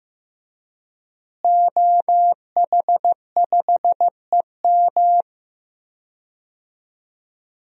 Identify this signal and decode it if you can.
Morse code, heard as a keyed tone, about 15 wpm, 700 Hz. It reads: OH5EM
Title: DXCC-100 Call Signs - 5 Characters - Repeated 3x - 15wpm